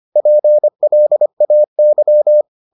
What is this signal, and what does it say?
Morse code, heard as a keyed tone, about 25 wpm, 600 Hz. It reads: PLAY